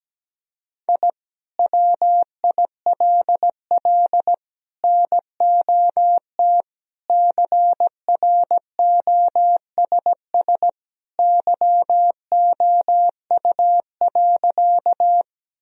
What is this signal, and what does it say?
Morse code, heard as a keyed tone, about 17 wpm, 700 Hz. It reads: I WILL NOT CROSS YOU.
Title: Sentences from Top 1000 Words - Review Part 1 - 17wpm